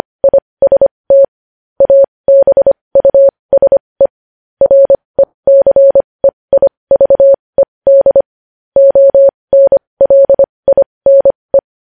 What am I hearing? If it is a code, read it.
Morse code, heard as a keyed tone, about 25 wpm, spaced as for 20 wpm, 550 Hz. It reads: IST ABUSE RECEIVED ONLINE
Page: news-headlines-in-morse-code-at-20-wpm